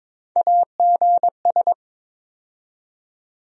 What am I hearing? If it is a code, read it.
Morse code, heard as a keyed tone, about 23 wpm, 700 Hz. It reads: AGS
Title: USA Call Sign Postfixes - 3 Letters 22wpm